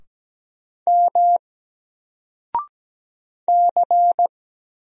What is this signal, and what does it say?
Morse code, heard as a keyed tone, about 17 wpm, 700 Hz. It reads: M C
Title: Single Letter-Number 17wpm